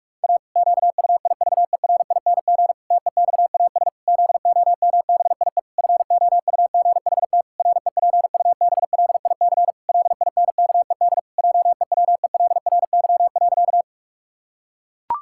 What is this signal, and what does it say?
Morse code, heard as a keyed tone, about 46 wpm, 700 Hz. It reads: A QUIVERING TEXAS ZOMBIE FOUGHT REPUBLIC LINKED JEWELRY.
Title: Pangram Challenge v2 - 45wpm